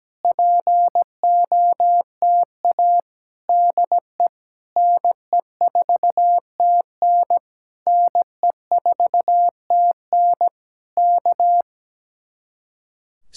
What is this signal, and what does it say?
Morse code, heard as a keyed tone, about 17 wpm, 700 Hz. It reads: POTA DE NE4TN NE4TN K
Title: Speed-Racing Common QSO Phrases - 17wpm